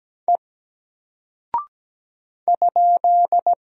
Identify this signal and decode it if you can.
Morse code, heard as a keyed tone, about 17 wpm, 700 Hz. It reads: E ?
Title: Single Character (Letter:Number:Punctuation:Prosign) Alt 17wpm